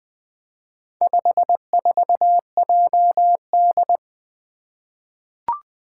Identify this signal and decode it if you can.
Morse code, heard as a keyed tone, about 20 wpm, 700 Hz. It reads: 54JD
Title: Four Challenging Letter-Number Combinations 20wpm